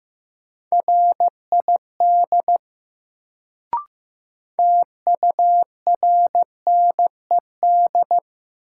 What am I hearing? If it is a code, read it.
Morse code, heard as a keyed tone, about 15 wpm, 700 Hz. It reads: RID TURNED
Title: Sets of 4 Words - New Word Drill - 15wpm